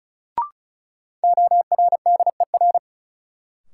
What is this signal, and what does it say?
Morse code, heard as a keyed tone, about 36 wpm, 700 Hz. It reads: ORDER
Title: Sets of 2 Words - New Word Drill - 35wpm